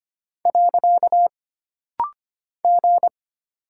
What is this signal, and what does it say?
Morse code, heard as a keyed tone, about 25 wpm, 700 Hz. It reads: . G
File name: Single Character (Letter:Number:Punctuation:Prosign) - Rapid-Fire - 25wpm